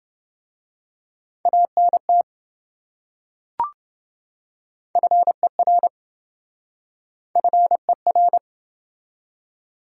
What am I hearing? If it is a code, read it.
Morse code, heard as a keyed tone, about 27 wpm, 700 Hz. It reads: ANT FER FER